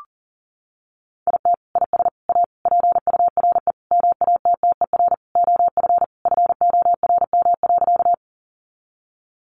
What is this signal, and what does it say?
Morse code, heard as a keyed tone, about 40 wpm, 700 Hz. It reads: IT IS A PURE MATTER OF FORM.
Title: Sentences from Top 1500 Words 40wpm